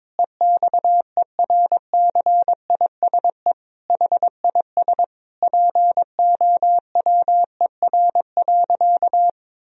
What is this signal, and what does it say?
Morse code, heard as a keyed tone, about 22 wpm, 700 Hz. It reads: EXERCISE HIS POWER.
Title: Sentences from Top 1000 Words - Review Part 4 - 22wpm